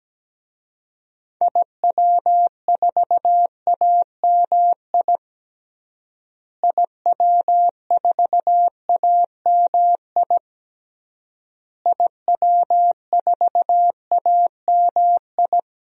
Morse code, heard as a keyed tone, about 17 wpm, 700 Hz. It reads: IW4AMI IW4AMI IW4AMI
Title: DXCC-100 Call Signs - 6 Characters - Repeated 3x - 17wpm